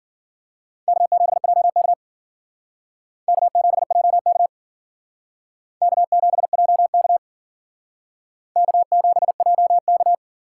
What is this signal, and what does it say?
Morse code, heard as a keyed tone, about 51 wpm, 700 Hz. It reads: K7JK K7JK K7JK K7JK